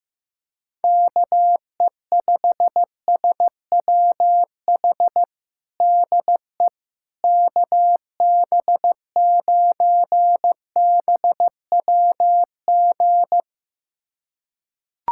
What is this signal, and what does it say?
Morse code, heard as a keyed tone, about 16 wpm, 700 Hz. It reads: KE5SWH DE KB9BWG
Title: USA Callsign DE Callsign - 6 Characters - 15wpm